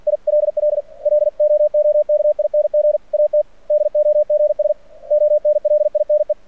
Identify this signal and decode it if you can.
Morse code, heard as a keyed tone, about 49 wpm, 600 Hz. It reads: ILL LOOKING AT DOGS ONLINE
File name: archived / CRSnet